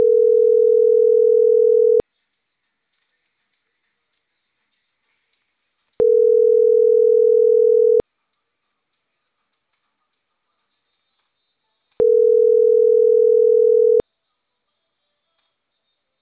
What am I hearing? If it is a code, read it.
Ringback tone (North America), 440 + 480 Hz.